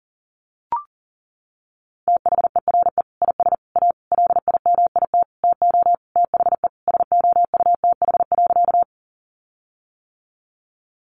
Morse code, heard as a keyed tone, about 40 wpm, 700 Hz. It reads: THERE IS A LIMIT TO THE SOUTH.
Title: Sentences from Top 1500 Words 40wpm